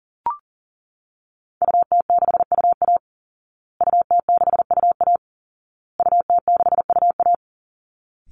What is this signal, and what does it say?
Morse code, heard as a keyed tone, about 40 wpm, 700 Hz. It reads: UT6UA UT6UA UT6UA
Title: Call Signs Repeated 2x - 40wpm